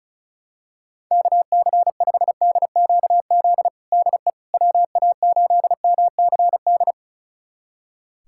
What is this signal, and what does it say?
Morse code, heard as a keyed tone, about 36 wpm, 700 Hz. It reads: KC5DQZ DE WA8MCD
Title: USA Callsign DE Callsign 35wpm